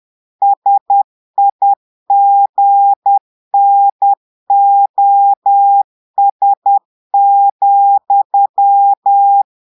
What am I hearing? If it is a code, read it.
Morse code, heard as a keyed tone, about 10 wpm, 800 Hz. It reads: SIGNOS,
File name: portugais texte en clair 2 vitesse 10 mots minutes